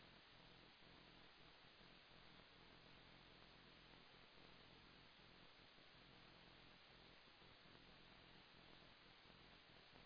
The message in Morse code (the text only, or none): none